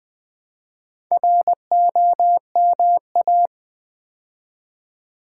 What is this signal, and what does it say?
Morse code, heard as a keyed tone, about 20 wpm, 700 Hz. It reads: ROMA
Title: Build A Word - 2 Letters At A Time - 8-Letter Words - 20wpm